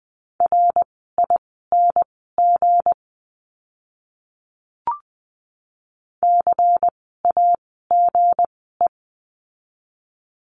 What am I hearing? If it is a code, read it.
Morse code, heard as a keyed tone, about 20 wpm, spaced as for 14 wpm, 700 Hz. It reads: RING CAGE
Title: 160 Lesson 26f - Letter g - Words 20wpm